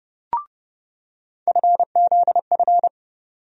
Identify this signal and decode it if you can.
Morse code, heard as a keyed tone, about 29 wpm, 700 Hz. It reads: FZF
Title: USA Call Sign Postfixes - 3 Letters 30wpm